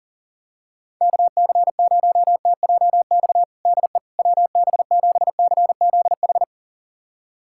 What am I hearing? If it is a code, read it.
Morse code, heard as a keyed tone, about 37 wpm, 700 Hz. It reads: KC0TJX DE WB7CZH
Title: USA Callsign DE Callsign 40wpm